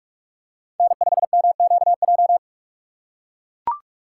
Morse code, heard as a keyed tone, about 43 wpm, 700 Hz. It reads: N5MQJ
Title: USA Call Signs - 5 Characters - 3x Character Spacing - 45wpm